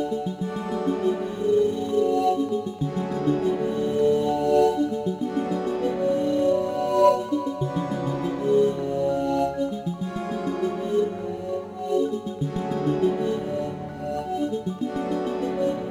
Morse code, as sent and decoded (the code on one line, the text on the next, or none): none